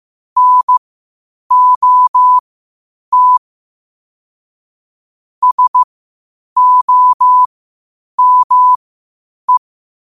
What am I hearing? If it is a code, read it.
Morse code, heard as a keyed tone, about 15 wpm, spaced as for 8 wpm, 1000 Hz. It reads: NOT SOME
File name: qotdfiles / 8wpm